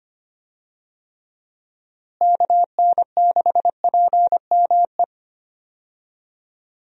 Morse code, heard as a keyed tone, about 25 wpm, 700 Hz. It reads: KN6PME
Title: USA Call Signs - 6 Characters - Repeated 2x - 25wpm